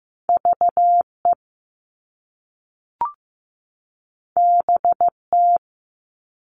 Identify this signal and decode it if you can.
Morse code, heard as a keyed tone, about 15 wpm, 700 Hz. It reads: VE BT
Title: All Cut Numbers - 2-Digit 15wpm